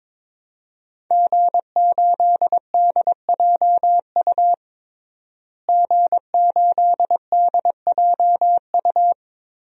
Morse code, heard as a keyed tone, about 22 wpm, 700 Hz. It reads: G8DJU G8DJU
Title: DXCC-100 Call Signs - 5 Characters - Repeated 3x - 22wpm